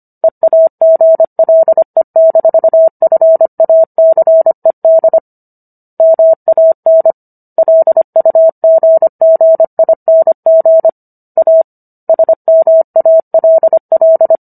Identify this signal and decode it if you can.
Morse code, heard as a keyed tone, about 25 wpm, 650 Hz. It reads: EAGLE-FACED MAN LUGGING A SMALL